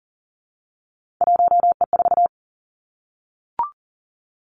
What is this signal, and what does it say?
Morse code, heard as a keyed tone, about 40 wpm, 700 Hz. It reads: 1E4
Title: Three Challenging Letter-Number Combinations 40wpm